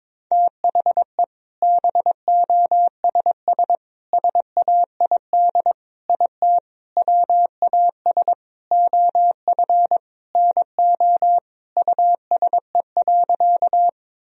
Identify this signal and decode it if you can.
Morse code, heard as a keyed tone, about 22 wpm, 700 Hz. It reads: THE BOSS SAID IT WAS OF NO USE.
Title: Sentences from Top 1000 Words - Review Part 4 - 22wpm